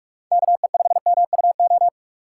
Morse code, heard as a keyed tone, about 46 wpm, 700 Hz. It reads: KE5MUO
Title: USA Call Signs - 3x Character Spacing - 45wpm